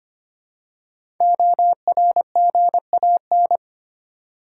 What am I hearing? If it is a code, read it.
Morse code, heard as a keyed tone, about 25 wpm, 700 Hz. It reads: ORGAN
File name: 5 Letter Words - First 3 Letters Given - 25wpm